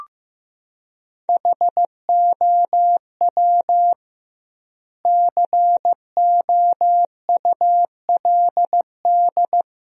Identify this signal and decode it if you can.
Morse code, heard as a keyed tone, about 15 wpm, 700 Hz. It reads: HOW COULD
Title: 2x Word-Spacing - Sentences from Top 300 Words - 15wpm